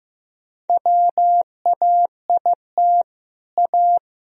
Morse code, heard as a keyed tone, about 15 wpm, 700 Hz. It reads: WAIT A
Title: Sentences from Top 1000 Words - Review Part 4 - 15wpm